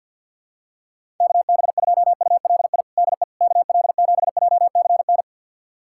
Morse code, heard as a keyed tone, about 50 wpm, 700 Hz. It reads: KB2ULI DE KL7JCN
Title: USA Callsign DE Callsign 50wpm